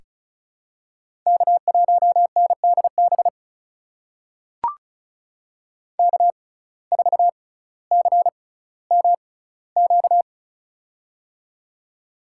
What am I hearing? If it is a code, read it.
Morse code, heard as a keyed tone, about 36 wpm, spaced as for 18 wpm, 700 Hz. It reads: K1NDB K4CMQ
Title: USA Call Signs - 5 Characters - 6x Character Spacing - 35wpm